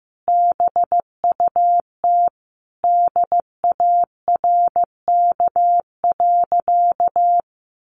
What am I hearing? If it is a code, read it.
Morse code, heard as a keyed tone, about 15 wpm, 700 Hz. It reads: BUT DARK.
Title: Sentences from Top 1400 Words 15wpm